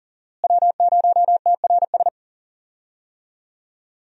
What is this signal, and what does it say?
Morse code, heard as a keyed tone, about 39 wpm, 700 Hz. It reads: W0TRS
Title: USA Call Signs - 5 Characters - Repeated 2x - 40wpm